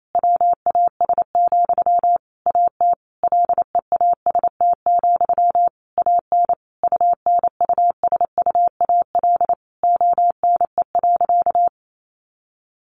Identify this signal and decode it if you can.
Morse code, heard as a keyed tone, about 28 wpm, 700 Hz. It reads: WAS, AT LEAST, AN UNUSUAL ONE.
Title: Sentences from Top 1400 Words 28wpm